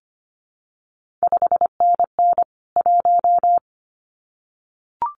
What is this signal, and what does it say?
Morse code, heard as a keyed tone, about 25 wpm, 700 Hz. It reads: 5NN 1
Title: CQ WW DX Contest Exchange Encore - 25wpm